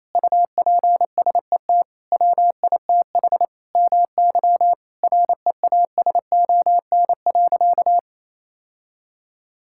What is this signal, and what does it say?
Morse code, heard as a keyed tone, about 28 wpm, 700 Hz. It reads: UPSET WITH MY REASON.
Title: Sentences from Top 1000 Words - Review Part 5 - 28wpm